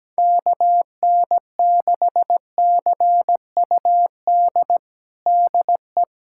Morse code, heard as a keyed tone, about 17 wpm, 700 Hz. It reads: KN6CUD DE
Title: USA Callsign DE Callsign - 6 Characters - 17wpm